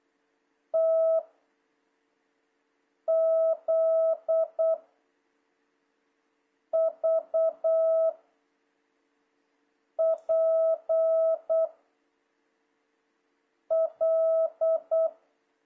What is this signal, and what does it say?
Morse code, heard as a keyed tone, about 8 wpm, 650 Hz. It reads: T Z V P L
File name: jh8cbh / 2026cw